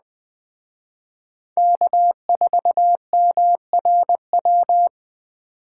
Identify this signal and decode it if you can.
Morse code, heard as a keyed tone, about 20 wpm, 700 Hz. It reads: K4MRW
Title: Speed Racing - USA Call Signs 17wpm